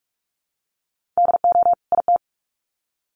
Morse code, heard as a keyed tone, about 44 wpm, 700 Hz. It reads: DO IT
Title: Sets of 2 Words 45wpm